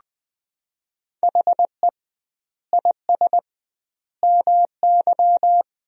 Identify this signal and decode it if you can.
Morse code, heard as a keyed tone, about 20 wpm, 700 Hz. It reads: HE IS MY
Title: 2x Word-Spacing - Sentences from Top 300 Words - 20wpm